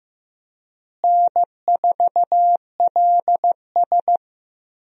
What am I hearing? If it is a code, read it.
Morse code, heard as a keyed tone, about 15 wpm, 700 Hz. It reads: N4LS
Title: USA Call Signs - 4 Characters - Repeated 3x - 15wpm